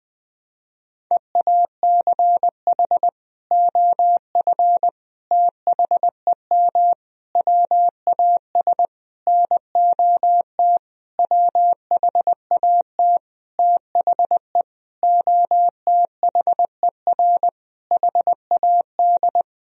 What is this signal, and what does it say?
Morse code, heard as a keyed tone, about 20 wpm, 700 Hz. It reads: EACH OF THEM WAS NOT WHAT THE OTHER HAD
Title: Sentences from Top 1000 Words - Review Part 1 - 20wpm